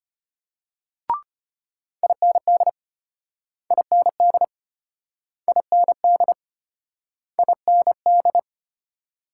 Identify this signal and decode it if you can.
Morse code, heard as a keyed tone, about 33 wpm, 700 Hz. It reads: IND IND IND IND